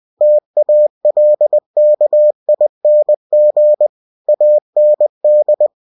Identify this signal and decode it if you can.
Morse code, heard as a keyed tone, about 20 wpm, 600 Hz. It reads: TALKING AND
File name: cw-books / winnie-the-pooh / winnie-20wpm-page-0107